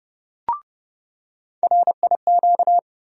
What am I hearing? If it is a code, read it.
Morse code, heard as a keyed tone, about 30 wpm, 700 Hz. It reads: RIQ
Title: USA Call Sign Postfixes - 3 Letters 30wpm